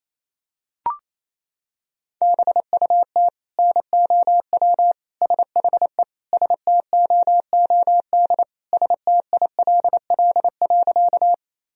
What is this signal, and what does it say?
Morse code, heard as a keyed tone, about 28 wpm, 700 Hz. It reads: BUT NOW SHE STOOD STILL.
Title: Sentences from Top 1000 Words - Review Part 2 - 28wpm